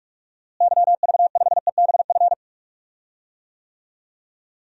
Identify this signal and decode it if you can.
Morse code, heard as a keyed tone, about 46 wpm, 700 Hz. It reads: YV5EBF
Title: DXCC-100 Call Signs - 6 Characters - Repeated 2x - 45wpm